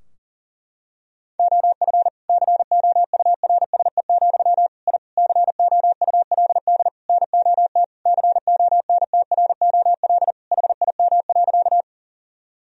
Morse code, heard as a keyed tone, about 40 wpm, 700 Hz. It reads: OF COURSE, I COULD NOT CONTROL HIM.
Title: Sentences from Top 1000 Words - Review Part 3 - 40wpm